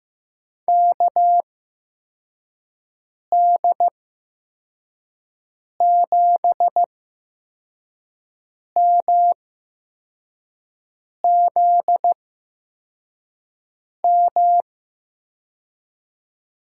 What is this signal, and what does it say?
Morse code, heard as a keyed tone, about 15 wpm, spaced as for 4 wpm, 700 Hz. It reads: KD7MZM
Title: USA Call Signs - 6 Characters - 8x Character Spacing - 15wpm